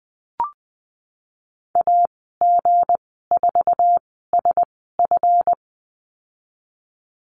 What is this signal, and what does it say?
Morse code, heard as a keyed tone, about 20 wpm, spaced as for 15 wpm, 700 Hz. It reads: AG4SF
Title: 186 Lesson 30g - Letter b - Call Signs 20wpm